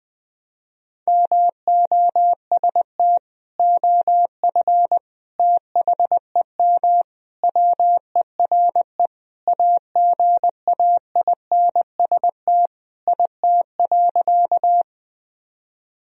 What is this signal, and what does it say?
Morse code, heard as a keyed tone, about 20 wpm, 700 Hz. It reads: MOST OF THEM WERE AGAINST IT.
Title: Sentences from Top 1000 Words - Review Part 1 - 20wpm